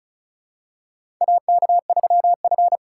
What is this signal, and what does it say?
Morse code, heard as a keyed tone, about 34 wpm, 700 Hz. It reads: AK3F
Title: USA Call Signs - 4 Characters - 4x Character Spacing - 35wpm